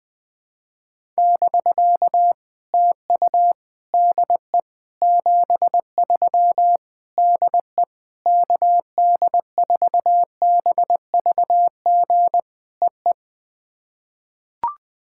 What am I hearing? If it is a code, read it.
Morse code, heard as a keyed tone, about 20 wpm, 700 Hz. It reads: <BK> TU DE 73 DE KD4BVG EE
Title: POTA QSO Practice - 20wpm